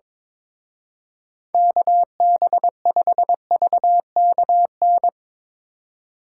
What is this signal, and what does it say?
Morse code, heard as a keyed tone, about 22 wpm, 690 Hz. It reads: KB5VKN
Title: Speed Racing - USA Call Signs 22wpm